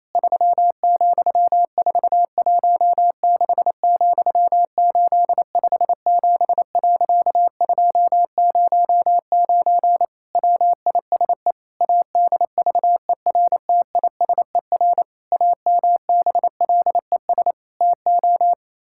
Morse code, heard as a keyed tone, about 28 wpm, 700 Hz. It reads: 3,416,857.209 WISE ADVERTISER AMBLES TO
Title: Pangram Challenge v2 - 28wpm